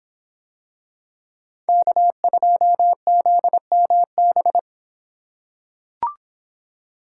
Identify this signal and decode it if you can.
Morse code, heard as a keyed tone, about 25 wpm, 700 Hz. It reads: K2ZMB